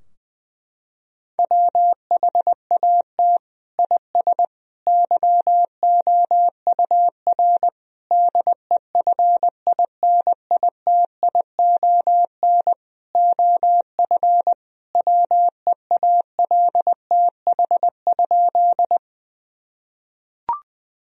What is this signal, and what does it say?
Morse code, heard as a keyed tone, about 20 wpm, 700 Hz. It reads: WHAT IS YOUR DEFINITION OF WEALTH?
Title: Sentences from Top 1000 Words - Review Part 4 - 20wpm